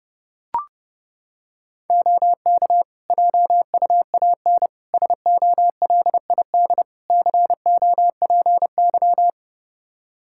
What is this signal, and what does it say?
Morse code, heard as a keyed tone, about 29 wpm, 700 Hz. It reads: OK JUAN SOLID COPY
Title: K1USN SST QSO Practice 30wpm